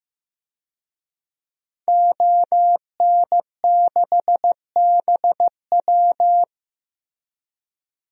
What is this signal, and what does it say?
Morse code, heard as a keyed tone, about 16 wpm, 700 Hz. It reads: ON6BW
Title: DXCC-100 Call Signs - 5 Characters - Repeated 2x - 15wpm